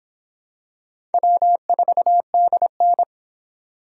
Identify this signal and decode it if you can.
Morse code, heard as a keyed tone, about 25 wpm, 700 Hz. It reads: W4DN